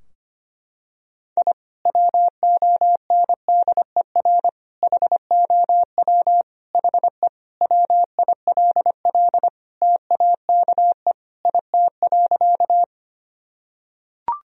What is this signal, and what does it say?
Morse code, heard as a keyed tone, about 25 wpm, 700 Hz. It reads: I WONDER HOW HE WILL TAKE IT.
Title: Sentences from Top 1000 Words - Review Part 3 - 25wpm